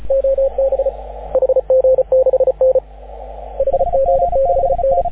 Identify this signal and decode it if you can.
Morse code, heard as a keyed tone, about 34 wpm, 530 Hz. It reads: OB HG6N HG6N
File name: AUDIO2019 / HG6N CW SSB MIX